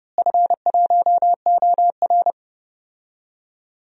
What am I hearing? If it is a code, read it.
Morse code, heard as a keyed tone, about 30 wpm, 700 Hz. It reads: F1OR
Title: DXCC-100 Call Signs - 4 Characters - Repeated 2x - 30wpm